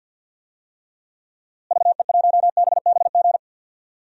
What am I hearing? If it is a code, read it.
Morse code, heard as a keyed tone, about 51 wpm, 700 Hz. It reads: VE1BBG
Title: DXCC-100 Call Signs - 6 Characters - Repeated 3x - 50wpm